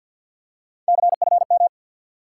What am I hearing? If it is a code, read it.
Morse code, heard as a keyed tone, about 47 wpm, 700 Hz. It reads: CFM
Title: Top 90 QSO Elements 50wpm